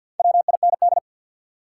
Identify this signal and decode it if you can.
Morse code, heard as a keyed tone, about 53 wpm, 700 Hz. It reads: WIND
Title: Sentences from Top 400 Words - New Word Drill - 50wpm